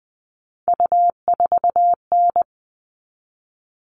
Morse code, heard as a keyed tone, about 20 wpm, 700 Hz. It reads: U4N
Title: All Cut Numbers - 3-Digit 20wpm